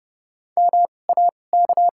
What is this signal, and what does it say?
Morse code, heard as a keyed tone, about 30 wpm, spaced as for 22 wpm, 700 Hz. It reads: MAK
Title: 2x Character Spacing - Top 300 Words - 30wpm